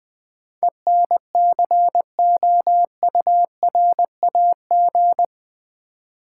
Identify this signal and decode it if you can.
Morse code, heard as a keyed tone, about 20 wpm, 700 Hz. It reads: ENCOURAG
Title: Build A Word - 4 Letters At A Time - 11-Letter Words - 20wpm